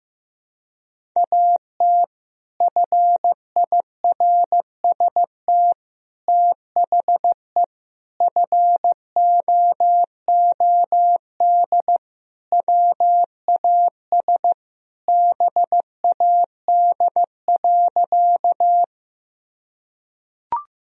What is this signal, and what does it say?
Morse code, heard as a keyed tone, about 16 wpm, 700 Hz. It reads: AT FIRST THE FOOD WAS BAD.
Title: Sentences from Top 1000 Words - Review Part 1 - 15wpm